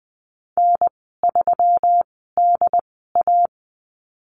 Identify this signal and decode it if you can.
Morse code, heard as a keyed tone, about 20 wpm, spaced as for 14 wpm, 700 Hz. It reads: N3DA
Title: 140 Lesson 23g - Letter f - Call Signs 20wpm